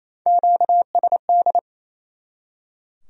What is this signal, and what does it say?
Morse code, heard as a keyed tone, about 35 wpm, 700 Hz. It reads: QSD